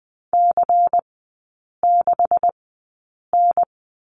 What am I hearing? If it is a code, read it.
Morse code, heard as a keyed tone, about 20 wpm, spaced as for 8 wpm, 700 Hz. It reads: C6N
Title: 127 Lesson 21e - Number 6 - Three Characters 20wpm